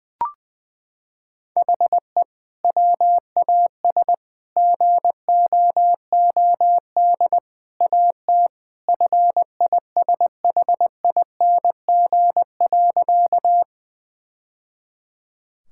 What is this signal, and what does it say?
Morse code, heard as a keyed tone, about 20 wpm, 700 Hz. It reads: HE WAS GOOD AT FISHING.
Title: Sentences from Top 1000 Words - Review Part 5 - 20wpm